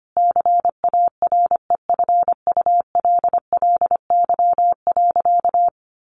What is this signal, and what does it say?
Morse code, heard as a keyed tone, about 25 wpm, 700 Hz. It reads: CAREFULLY.
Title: Sentences from Top 1300 Words 25wpm